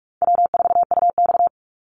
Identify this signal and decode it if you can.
Morse code, heard as a keyed tone, about 44 wpm, 700 Hz. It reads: W4UX
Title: Call Signs - 45wpm